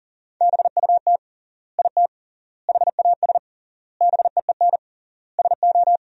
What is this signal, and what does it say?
Morse code, heard as a keyed tone, about 40 wpm, 700 Hz. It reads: BUT IT HAS BEEN SO
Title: 3x Word-Spacing - Sentences from Top 100 Words - 40wpm